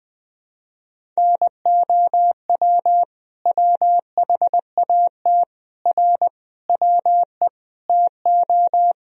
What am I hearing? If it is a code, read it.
Morse code, heard as a keyed tone, about 21 wpm, 700 Hz. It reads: NOW WHAT R WE TO
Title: Sentences from Top 100 Words and Abbreviations 20wpm